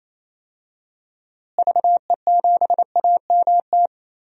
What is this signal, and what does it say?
Morse code, heard as a keyed tone, about 28 wpm, 700 Hz. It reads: VE7AMT
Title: DXCC-100 Call Signs - 6 Characters - Repeated 2x - 28wpm